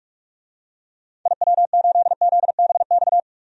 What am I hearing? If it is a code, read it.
Morse code, heard as a keyed tone, about 46 wpm, 700 Hz. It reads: IW8ZBX
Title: DXCC-100 Call Signs - 6 Characters - 45wpm